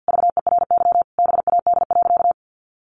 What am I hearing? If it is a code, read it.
Morse code, heard as a keyed tone, about 50 wpm, 700 Hz. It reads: VERY BAD.
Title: Sentences from Top 1200 Words 50wpm